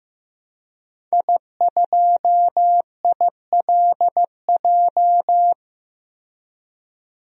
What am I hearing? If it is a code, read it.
Morse code, heard as a keyed tone, about 15 wpm, 700 Hz. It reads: I2ILJ
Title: DXCC-100 Call Signs - 5 Characters - Repeated 2x - 15wpm